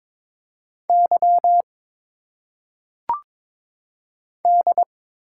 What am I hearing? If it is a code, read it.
Morse code, heard as a keyed tone, about 22 wpm, 700 Hz. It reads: Y D
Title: Single Letters - Phonetic Pronunciation 22wpm